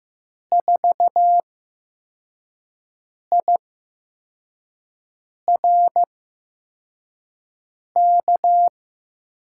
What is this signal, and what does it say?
Morse code, heard as a keyed tone, about 15 wpm, spaced as for 4 wpm, 700 Hz. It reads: 4IRK
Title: USA Call Signs - 6 Characters - 8x Character Spacing - 15wpm